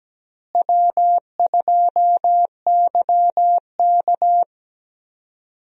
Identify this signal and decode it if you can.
Morse code, heard as a keyed tone, about 17 wpm, 700 Hz. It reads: W2YK
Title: USA Call Signs - 4 Characters - Repeated 2x - 17wpm